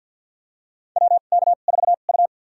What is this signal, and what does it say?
Morse code, heard as a keyed tone, about 53 wpm, spaced as for 37 wpm, 700 Hz. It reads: WK4U
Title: USA Call Signs - 4 Characters - 2x Character Spacing - 50wpm